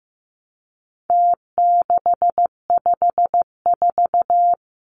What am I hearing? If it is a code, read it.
Morse code, heard as a keyed tone, about 15 wpm, 700 Hz. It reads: T654
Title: Numbers - Contest Style - 4-Digit 15wpm